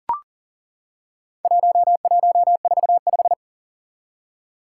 Morse code, heard as a keyed tone, about 40 wpm, 700 Hz. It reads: 1145
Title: Four Challenging Letter-Number Combinations 40wpm